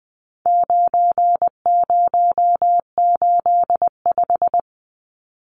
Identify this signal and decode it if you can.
Morse code, heard as a keyed tone, about 20 wpm, 700 Hz. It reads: 9085
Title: Numbers - 4-Digit 20wpm